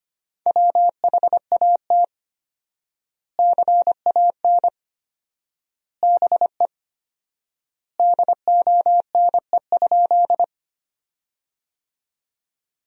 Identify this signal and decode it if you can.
Morse code, heard as a keyed tone, about 25 wpm, 700 Hz. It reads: WHAT CAN BE DONE?
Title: 4x Word-Spacing - Sentences from Top 500 Words - 25wpm